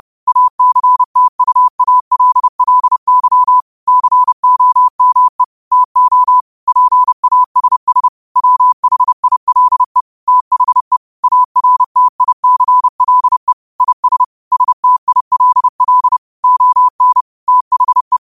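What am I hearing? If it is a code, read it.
Morse code, heard as a keyed tone, about 30 wpm, 1000 Hz. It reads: ACTUALLY COME TO PASS WHILE THE ARTICLE IS STILL ON THE
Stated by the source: Morse quote of the day at 30 WPM.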